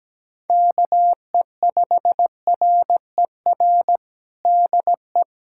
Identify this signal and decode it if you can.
Morse code, heard as a keyed tone, about 17 wpm, 700 Hz. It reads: KE5RER DE
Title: USA Callsign DE Callsign 17wpm